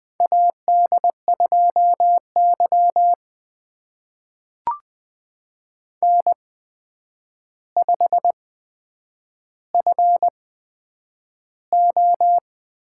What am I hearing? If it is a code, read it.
Morse code, heard as a keyed tone, about 20 wpm, spaced as for 9 wpm, 700 Hz. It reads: AD2Y N5FO
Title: USA Call Signs - 4 Characters - 8x Character Spacing - 20wpm